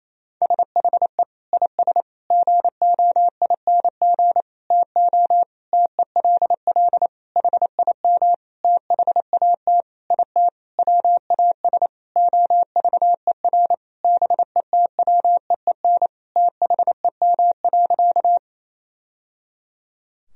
Morse code, heard as a keyed tone, about 27 wpm, 700 Hz. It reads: SHE IS GOING TO TELL HIM THAT IT WAS OVER BETWEEN THEM.
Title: Sentences from Top 1000 Words - Review Part 1 - 28wpm